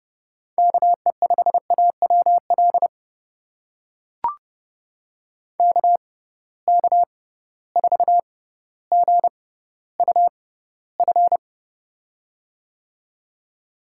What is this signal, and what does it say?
Morse code, heard as a keyed tone, about 30 wpm, spaced as for 15 wpm, 700 Hz. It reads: KE5AWL KK4GUF
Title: USA Call Signs - 6 Characters - 6x Character Spacing - 30wpm